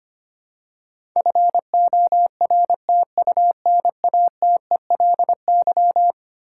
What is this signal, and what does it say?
Morse code, heard as a keyed tone, about 25 wpm, 700 Hz. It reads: FORTUNATELY
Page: morse-code-ninja-practice-25wpm